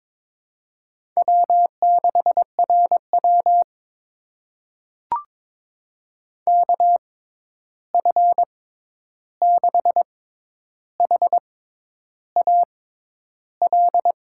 Morse code, heard as a keyed tone, about 22 wpm, spaced as for 10 wpm, 700 Hz. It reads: W6RW KF6HAL
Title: USA Call Signs - 6x Character Spacing - 22wpm